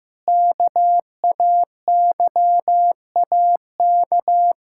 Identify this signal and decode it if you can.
Morse code, heard as a keyed tone, about 15 wpm, 700 Hz. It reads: KAYAK
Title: Pangram Challenge v2 - 15wpm